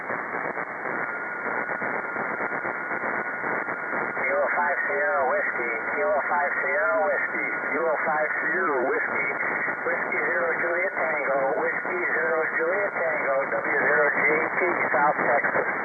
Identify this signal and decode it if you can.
FT8, with speech.